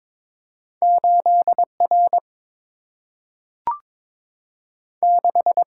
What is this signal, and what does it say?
Morse code, heard as a keyed tone, about 22 wpm, 700 Hz. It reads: 8 R 6
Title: Single-Letter Single-Number 22wpm